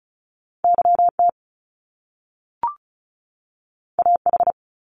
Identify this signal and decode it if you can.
Morse code, heard as a keyed tone, about 36 wpm, 700 Hz. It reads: YT AH
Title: Top Two Letter Combinations in Words 35wpm